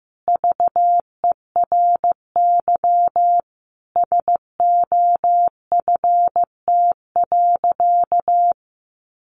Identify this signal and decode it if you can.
Morse code, heard as a keyed tone, about 15 wpm, 700 Hz. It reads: VERY SOFT.
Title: Easy Phrases 15wpm